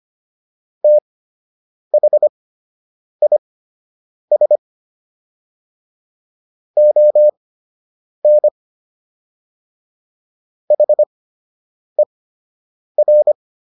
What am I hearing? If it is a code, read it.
Morse code, heard as a keyed tone, about 25 wpm, spaced as for 8 wpm, 600 Hz. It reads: THIS ON HER